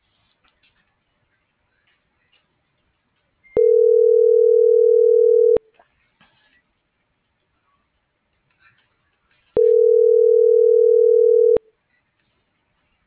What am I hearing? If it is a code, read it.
Ringback tone (North America), 440 + 480 Hz.